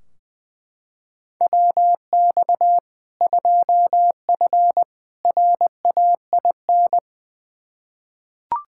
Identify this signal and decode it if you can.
Morse code, heard as a keyed tone, about 20 wpm, 700 Hz. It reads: WX 2F RAIN
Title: K1USN SST QSO Practice 20wpm